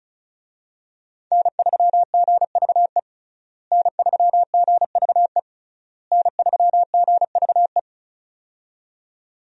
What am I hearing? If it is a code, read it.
Morse code, heard as a keyed tone, about 34 wpm, 700 Hz. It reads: N3GVE N3GVE N3GVE
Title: USA Call Signs - 5 Characters - Repeated 3x - 35wpm